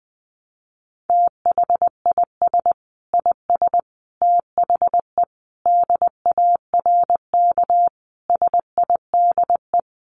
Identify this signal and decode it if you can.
Morse code, heard as a keyed tone, about 20 wpm, 700 Hz. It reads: THIS IS THE DARK SIDE
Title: Sentences from Top 1200 Words 20wpm